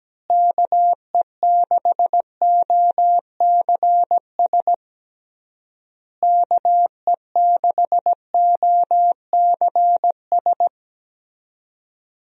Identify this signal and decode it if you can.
Morse code, heard as a keyed tone, about 17 wpm, 700 Hz. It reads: KE6OCS KE6OCS
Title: USA Call Signs - 6 Characters - Repeated 3x - 17wpm